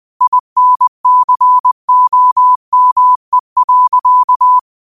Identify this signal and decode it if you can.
Morse code, heard as a keyed tone, about 20 wpm, 1000 Hz. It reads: INCOME.
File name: qotdfiles / 20wpm